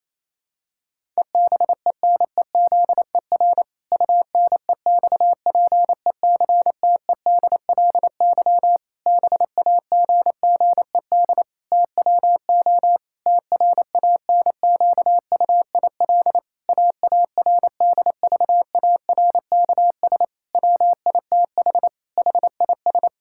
Morse code, heard as a keyed tone, about 28 wpm, 700 Hz. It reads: EBENEZER UNEXPECTEDLY BAGGED TWO TRANQUIL AARDVARKS WITH HIS
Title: Pangram Challenge v2 - 28wpm